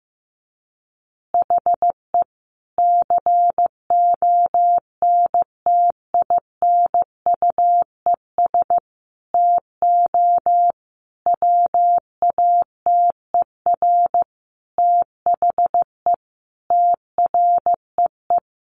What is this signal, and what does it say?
Morse code, heard as a keyed tone, about 15 wpm, 700 Hz. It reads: HE CONTINUES TO WATER THE TREE
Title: Easy Phrases 15wpm